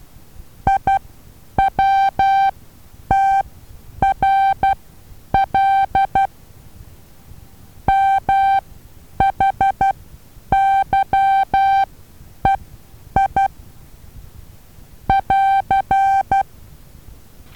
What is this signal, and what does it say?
Morse code, heard as a keyed tone, about 12 wpm, spaced as for 9 wpm, 780 Hz. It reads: IWTRL MHYEI <AR>
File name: Taviro HA6VA B